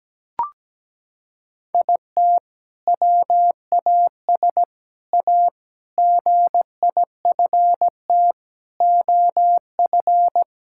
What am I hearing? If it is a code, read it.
Morse code, heard as a keyed tone, about 17 wpm, 700 Hz. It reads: IT WAS A GIFT OF
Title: Sentences from Top 1000 Words - Review Part 6 - 17wpm